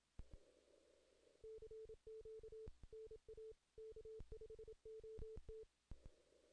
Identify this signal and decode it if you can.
Morse code, heard as a keyed tone, about 27 wpm, 440 Hz. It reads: CQ NA K5OT
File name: 2016 JAN CW NAQP / 2016 Jan naqp k5ot - 0-10